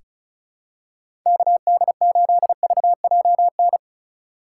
Morse code, heard as a keyed tone, about 36 wpm, 700 Hz. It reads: KD8VJN
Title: USA Call Signs - 6 Characters - 2x Character Spacing - 35wpm